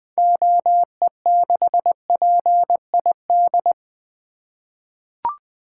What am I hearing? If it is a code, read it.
Morse code, heard as a keyed tone, about 20 wpm, 700 Hz. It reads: OE6PID
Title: 2023 SOTA Activator Call Signs - 20wpm